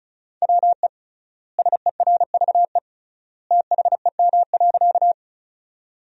Morse code, heard as a keyed tone, about 34 wpm, 700 Hz. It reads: WE SERVE THEM.
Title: 3x Word-Spacing - Sentences from Top 400 Words - 35wpm